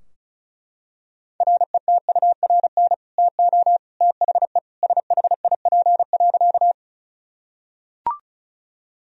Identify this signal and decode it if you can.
Morse code, heard as a keyed tone, about 36 wpm, 700 Hz. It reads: RETURN TO THE SHIP.
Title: Sentences from Top 1000 Words - Review Part 2 - 35wpm